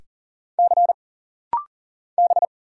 Morse code, heard as a keyed tone, about 40 wpm, 700 Hz. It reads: C B
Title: Single Letters - Rapid-Fire 40wpm